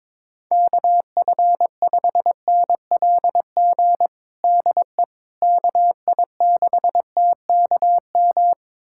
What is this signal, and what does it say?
Morse code, heard as a keyed tone, about 22 wpm, 700 Hz. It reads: KF5NLG DE KI6TKM
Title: USA Callsign DE Callsign - 6 Characters - 22wpm